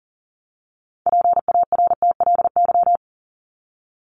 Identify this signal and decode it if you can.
Morse code, heard as a keyed tone, about 40 wpm, 700 Hz. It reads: PARTLY
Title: LY Suffix - Top 100 Words with Base Words 40wpm